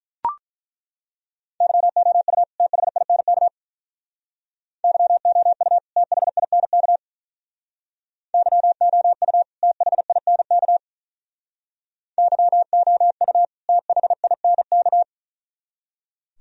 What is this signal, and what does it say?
Morse code, heard as a keyed tone, about 43 wpm, 700 Hz. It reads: YOU THINK YOU THINK YOU THINK YOU THINK
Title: Speed-Racing - Sets of 2 Words 35wpm